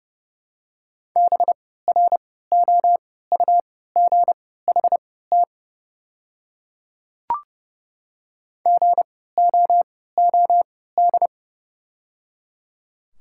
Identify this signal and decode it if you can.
Morse code, heard as a keyed tone, about 30 wpm, spaced as for 17 wpm, 700 Hz. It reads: BROUGHT GOOD
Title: 3x Character Spacing - Top 500 Words - 30wpm